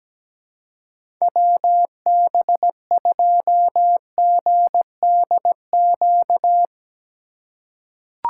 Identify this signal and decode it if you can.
Morse code, heard as a keyed tone, about 17 wpm, 700 Hz. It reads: WB2GDQ
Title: USA Call Signs - 2x Character Spacing - 17wpm